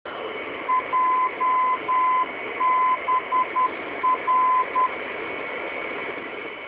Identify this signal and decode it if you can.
Morse code, heard as a keyed tone, about 10 wpm, 1020 Hz, with radio noise. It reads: JBR